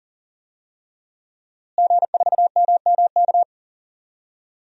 Morse code, heard as a keyed tone, about 40 wpm, 700 Hz. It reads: G4MMK
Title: DXCC-100 Call Signs - 5 Characters - Repeated 3x - 40wpm